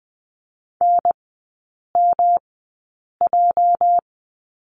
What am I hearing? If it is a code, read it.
Morse code, heard as a keyed tone, about 20 wpm, spaced as for 8 wpm, 700 Hz. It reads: NMJ
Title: 205 Lesson 33e - Letter j - Three Characters 20wpm